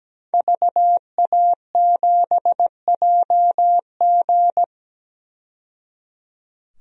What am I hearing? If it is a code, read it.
Morse code, heard as a keyed tone, about 17 wpm, 700 Hz. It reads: VA7JG
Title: DXCC-100 Call Signs - 5 Characters - 17wpm